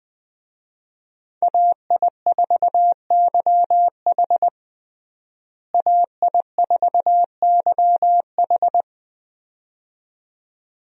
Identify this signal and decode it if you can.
Morse code, heard as a keyed tone, about 21 wpm, 700 Hz. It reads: AI4YH AI4YH
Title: USA Call Signs - 5 Characters - Repeated 2x - 20wpm